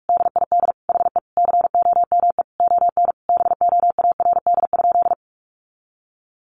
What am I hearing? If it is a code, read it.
Morse code, heard as a keyed tone, about 44 wpm, 700 Hz. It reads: DID HE COME ON BOARD?
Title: Sentences from Top 700 Words 45wpm